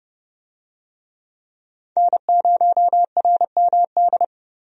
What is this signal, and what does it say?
Morse code, heard as a keyed tone, about 32 wpm, 700 Hz. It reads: N0RMD
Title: USA Call Signs - 5 Characters - Repeated 3x - 30wpm